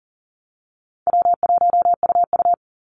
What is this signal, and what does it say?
Morse code, heard as a keyed tone, about 38 wpm, 700 Hz. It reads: W1UU
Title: Easy Call Signs - 40wpm